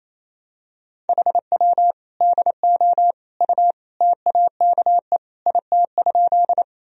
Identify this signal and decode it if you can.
Morse code, heard as a keyed tone, about 28 wpm, 700 Hz. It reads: HW DO U TAKE IT?
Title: Sentences from Top 200 Words and Abbreviations 28wpm